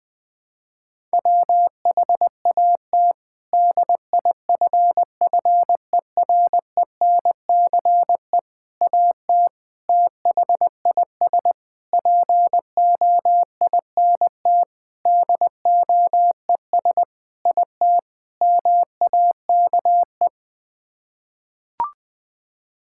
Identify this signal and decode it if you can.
Morse code, heard as a keyed tone, about 21 wpm, 700 Hz. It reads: WHAT DIFFERENCE AT THIS POINT DOES IT MAKE
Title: Sets of 8 Words - 20wpm